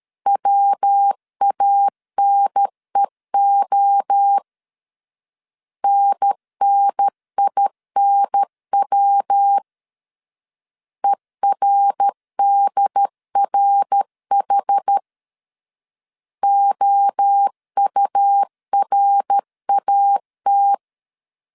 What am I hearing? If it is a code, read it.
Morse code, heard as a keyed tone, about 13 wpm, 800 Hz. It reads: WANEO NNINW ERDRH OURAT
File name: CW / DET / 1 3 1500C S C 12wpm 2750